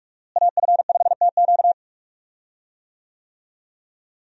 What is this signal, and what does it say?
Morse code, heard as a keyed tone, about 42 wpm, 700 Hz. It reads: AF5TQ
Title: USA Call Signs - 5 Characters - Repeated 2x - 45wpm